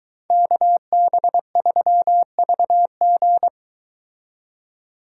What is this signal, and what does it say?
Morse code, heard as a keyed tone, about 23 wpm, 700 Hz. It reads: KB3VG